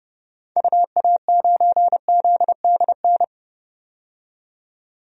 Morse code, heard as a keyed tone, about 31 wpm, 700 Hz. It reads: UA9ZDN
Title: DXCC-100 Call Signs - 6 Characters - 30wpm